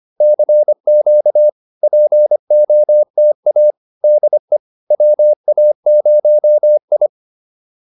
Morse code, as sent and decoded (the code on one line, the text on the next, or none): -.-. --.- / .--. --- - .- / -.. / .-- .- ----- ..
CQ POTA DE WA0I